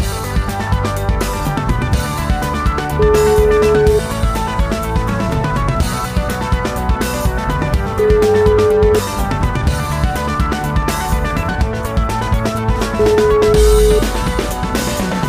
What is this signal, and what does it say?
Ringback tone (Europe, 425 Hz), with noise.